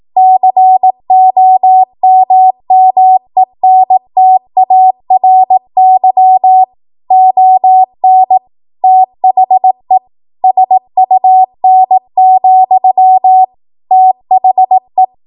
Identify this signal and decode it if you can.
Morse code, heard as a keyed tone, about 18 wpm, 750 Hz. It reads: COMMENTARY ON THE SUN, THE
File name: Morse / 18WPM